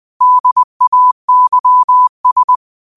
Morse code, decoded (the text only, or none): DAYS